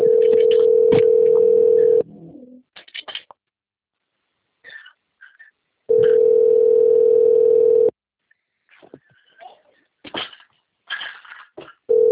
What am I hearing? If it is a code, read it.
Ringback tone (North America), 440 + 480 Hz.